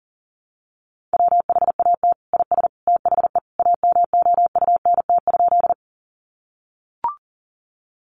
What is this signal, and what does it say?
Morse code, heard as a keyed tone, about 40 wpm, 700 Hz. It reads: WHAT IS THE AMOUNT?
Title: Sentences from Top 700 Words 40wpm